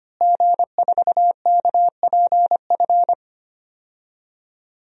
Morse code, heard as a keyed tone, about 25 wpm, 700 Hz. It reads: G4KPF
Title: DXCC-100 Call Signs - 5 Characters - Repeated 3x - 25wpm